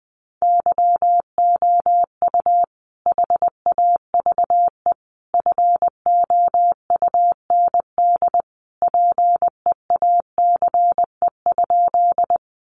Morse code, heard as a keyed tone, about 20 wpm, 700 Hz. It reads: YOU HAVE FOUND PEACE?
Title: Sentences from Top 1500 Words 20wpm